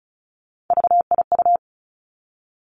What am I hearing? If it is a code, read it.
Morse code, heard as a keyed tone, about 36 wpm, 700 Hz. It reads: VIU
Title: Three Challenging Letter-Number Combinations 35wpm